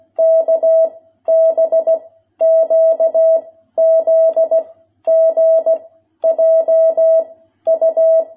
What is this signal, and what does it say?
Morse code, heard as a keyed tone, about 16 wpm, 650 Hz. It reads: K B Q Z G J U
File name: jh8cbh / 2021cw